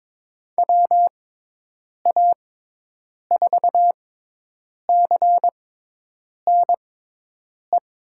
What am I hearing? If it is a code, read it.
Morse code, heard as a keyed tone, about 22 wpm, spaced as for 8 wpm, 700 Hz. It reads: WA4CNE
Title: USA Call Signs - 6 Characters - 6x Character Spacing - 22wpm